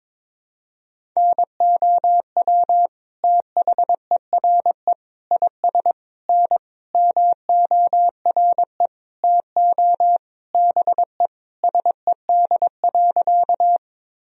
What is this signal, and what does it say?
Morse code, heard as a keyed tone, about 22 wpm, 700 Hz. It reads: NOW THERE IS N MORE TO BE SED.
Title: Sentences from Top 100 Words and Abbreviations 22wpm